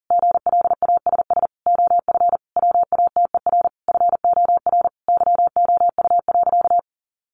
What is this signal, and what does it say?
Morse code, heard as a keyed tone, about 40 wpm, 700 Hz. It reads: GLASS OF WATER FOR YOU.
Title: Sentences from Top 1300 Words 40wpm